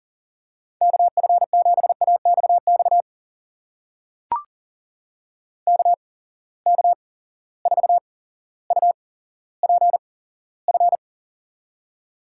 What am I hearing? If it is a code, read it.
Morse code, heard as a keyed tone, about 43 wpm, spaced as for 17 wpm, 700 Hz. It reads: KF7AXX KK4UPF
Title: USA Call Signs - 6 Characters - 8x Character Spacing - 40wpm